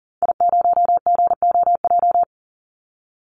Morse code, heard as a keyed tone, about 40 wpm, 700 Hz. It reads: I0GOJ
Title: Call Signs Repeated 2x - 40wpm